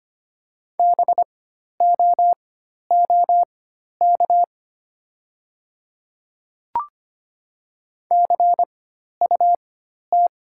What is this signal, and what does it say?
Morse code, heard as a keyed tone, about 25 wpm, spaced as for 12 wpm, 690 Hz. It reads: BOOK CUT
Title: 4x Character Spacing - Top 500 Words - 25wpm